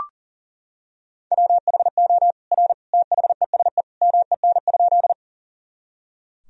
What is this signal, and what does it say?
Morse code, heard as a keyed tone, about 40 wpm, 700 Hz. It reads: WHO R THESE MEN?
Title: Sentences from Top 200 Words and Abbreviations 40wpm